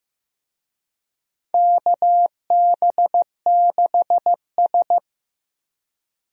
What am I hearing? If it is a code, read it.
Morse code, heard as a keyed tone, about 15 wpm, 700 Hz. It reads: KB6S
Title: USA Call Signs - 4 Characters - Repeated 3x - 15wpm